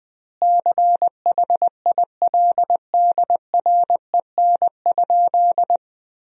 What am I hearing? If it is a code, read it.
Morse code, heard as a keyed tone, about 20 wpm, 700 Hz. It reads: CHILDREN?
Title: 2x Word-Spacing - Sentences from Top 300 Words - 20wpm